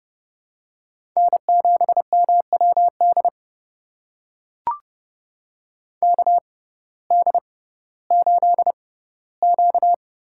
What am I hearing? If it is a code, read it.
Morse code, heard as a keyed tone, about 32 wpm, spaced as for 17 wpm, 700 Hz. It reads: N7MWD KD8Q
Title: USA Call Signs - 6x Character Spacing - 30wpm